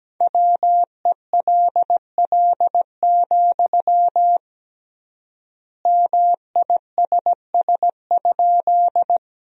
Morse code, heard as a keyed tone, about 17 wpm, 700 Hz. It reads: WELL, MISS?
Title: 3x Word-Spacing - Sentences from Top 500 Words - 17wpm